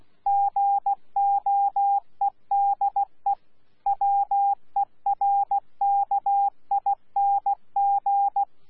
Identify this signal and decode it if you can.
Morse code, heard as a keyed tone, about 16 wpm, 800 Hz. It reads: GOEDE WERKING
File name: Z PI4VRZ / PI4VRZ 14-jun-2025